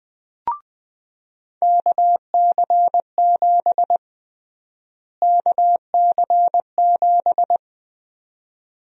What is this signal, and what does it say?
Morse code, heard as a keyed tone, about 21 wpm, 700 Hz. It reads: KC7 KC7
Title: USA Call Sign Prefixes - Repeated 2x - 20wpm